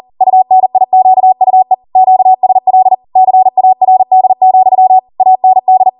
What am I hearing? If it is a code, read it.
Morse code, heard as a keyed tone, about 40 wpm, 750 Hz. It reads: UNIQUE QSL CARD, AND